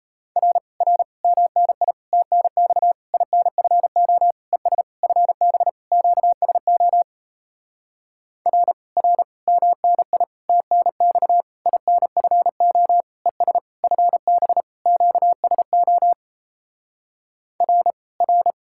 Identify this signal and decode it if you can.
Morse code, heard as a keyed tone, about 37 wpm, 700 Hz. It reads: R R MNI TNX INFO ES FB QSO R R MNI TNX INFO ES FB QSO R R